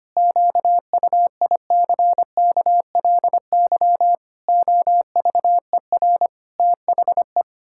Morse code, heard as a keyed tone, about 25 wpm, 700 Hz. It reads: QUICKLY OVER THE
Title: Pangram Challenge v2 - 25wpm